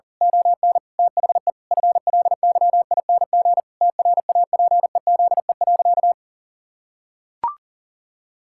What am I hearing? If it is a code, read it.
Morse code, heard as a keyed tone, about 40 wpm, 700 Hz. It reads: ON THE FLYING TRAPEZE.
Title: Pangram Challenge v2 - 40wpm